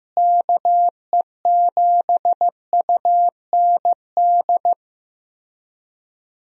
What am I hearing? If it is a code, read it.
Morse code, heard as a keyed tone, about 15 wpm, 700 Hz. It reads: KE7UND
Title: USA Call Signs - 6 Characters - Repeated 3x - 15wpm